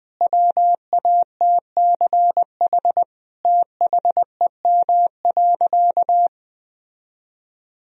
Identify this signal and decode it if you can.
Morse code, heard as a keyed tone, about 20 wpm, 700 Hz. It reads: WATCH THEM.
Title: Sentences from Top 1000 Words - Review Part 1 - 20wpm